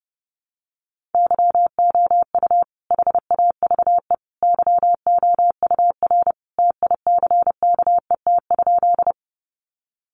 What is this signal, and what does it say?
Morse code, heard as a keyed tone, about 30 wpm, 700 Hz. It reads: YOU HAVE YOUR TICKET?
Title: Sentences from Top 1400 Words 30wpm